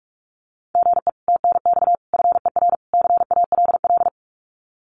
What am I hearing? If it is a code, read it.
Morse code, heard as a keyed tone, about 44 wpm, 700 Hz. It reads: GE TNX FER CALL
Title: Common QSO Phrases Encore 45wpm